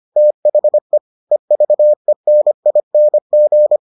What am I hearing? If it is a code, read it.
Morse code, heard as a keyed tone, about 25 wpm, 600 Hz. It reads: THE EVENING